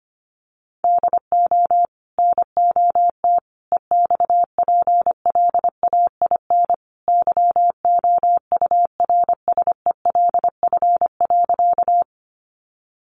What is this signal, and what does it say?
Morse code, heard as a keyed tone, about 25 wpm, 700 Hz. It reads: DO NOT EXPLAIN YOURSELF.
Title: Sentences from Top 1100 Words 25wpm